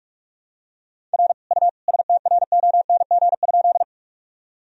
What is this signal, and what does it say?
Morse code, heard as a keyed tone, about 43 wpm, 690 Hz. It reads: R U STRONG?
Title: Sentences from Top 500 Words and Abbreviations 45wpm